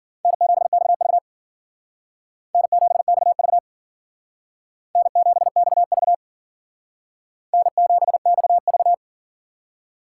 Morse code, heard as a keyed tone, about 53 wpm, 700 Hz. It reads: N7XV N7XV N7XV N7XV